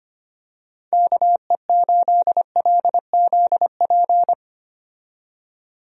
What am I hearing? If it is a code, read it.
Morse code, heard as a keyed tone, about 25 wpm, 700 Hz. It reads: KE8LZP
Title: USA Call Signs - 6 Characters - 25wpm